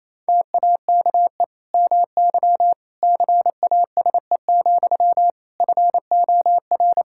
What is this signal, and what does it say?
Morse code, heard as a keyed tone, about 28 wpm, 700 Hz. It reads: TAKE MY CASE, FOR
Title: Sentences from Top 1000 Words - Review Part 5 - 28wpm